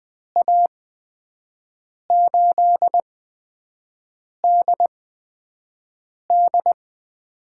Morse code, heard as a keyed tone, about 20 wpm, spaced as for 5 wpm, 700 Hz. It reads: A8DD
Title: USA Call Signs - 8x Character Spacing - 20wpm